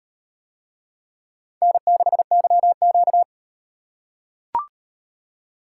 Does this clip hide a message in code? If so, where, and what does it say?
Yes: Morse code from 1.6 to 3.2 s, heard as a keyed tone, about 38 wpm, 700 Hz. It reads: N6YQ